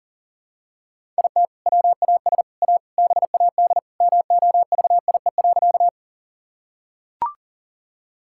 Morse code, heard as a keyed tone, about 40 wpm, 700 Hz. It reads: IT WAS A BAD MOVIE.
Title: Sentences from Top 1000 Words - Review Part 5 - 40wpm